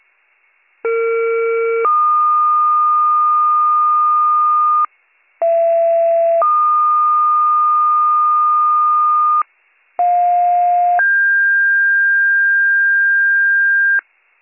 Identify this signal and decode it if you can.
Two-tone page, heard as a one-second tone, then a three-second tone, about 460 then 1220 Hz, about 670 then 1220 Hz, about 710 then 1600 Hz. It is heard from 0.8 to 14.0 s.